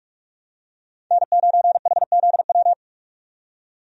Morse code, heard as a keyed tone, about 46 wpm, 700 Hz. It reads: N9HZW
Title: USA Call Signs - 5 Characters - 45wpm